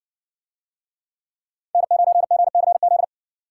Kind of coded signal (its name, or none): Morse code